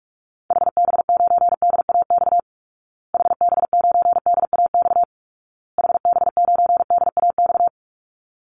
Morse code, heard as a keyed tone, about 44 wpm, 700 Hz. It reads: HB9DAX HB9DAX HB9DAX
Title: Call Signs Repeated 2x - 45wpm